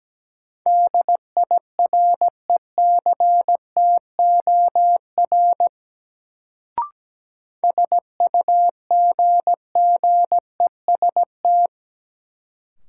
Morse code, heard as a keyed tone, about 17 wpm, 700 Hz. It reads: DIRECTOR SUGGEST
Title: Sentences from Top 1300 Words - New Word Drill - 17wpm